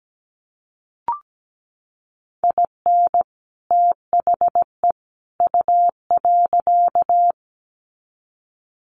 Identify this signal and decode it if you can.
Morse code, heard as a keyed tone, about 17 wpm, 700 Hz. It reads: IN THE U.
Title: Sets of 3 Words 17wpm